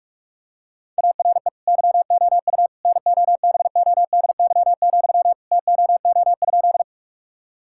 Morse code, heard as a keyed tone, about 41 wpm, 700 Hz. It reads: ARE YOU NOBODY, TOO?
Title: Sentences from Top 1600 Words 45wpm